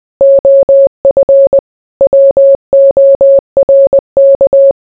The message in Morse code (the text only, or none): OF WORK